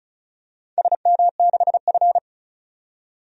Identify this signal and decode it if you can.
Morse code, heard as a keyed tone, about 36 wpm, 700 Hz. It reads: SM6F
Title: DXCC-100 Call Signs - 4 Characters - 35wpm